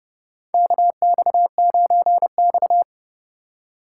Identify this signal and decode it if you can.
Morse code, heard as a keyed tone, about 29 wpm, 700 Hz. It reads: KX9X
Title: USA Call Signs - 4 Characters - 3x Character Spacing - 30wpm